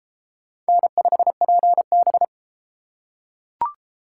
Morse code, heard as a keyed tone, about 32 wpm, 700 Hz. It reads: N5PB